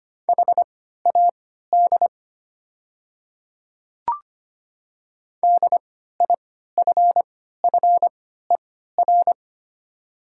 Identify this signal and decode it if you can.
Morse code, heard as a keyed tone, about 25 wpm, spaced as for 14 wpm, 700 Hz. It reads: HAD DIFFER
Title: 3x Character Spacing - Top 500 Words - 25wpm